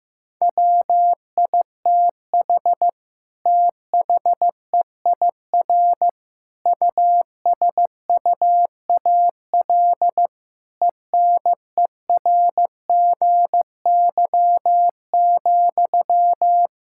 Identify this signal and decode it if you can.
Morse code, heard as a keyed tone, about 15 wpm, 700 Hz. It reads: WITH THEIR USUAL ENERGY,
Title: Sentences from Top 1000 Words - Review Part 6 - 15wpm